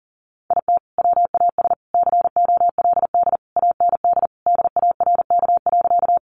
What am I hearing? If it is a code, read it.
Morse code, heard as a keyed tone, about 40 wpm, 700 Hz. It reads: IT WAS COLD AND DARK.
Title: Sentences from Top 500 Words 40wpm